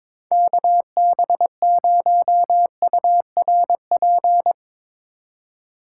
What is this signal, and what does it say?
Morse code, heard as a keyed tone, about 22 wpm, 700 Hz. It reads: KB0URP
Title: USA Call Signs - 6 Characters - 2x Character Spacing - 22wpm